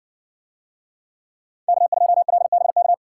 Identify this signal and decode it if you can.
Morse code, heard as a keyed tone, about 55 wpm, 700 Hz. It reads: K2LBK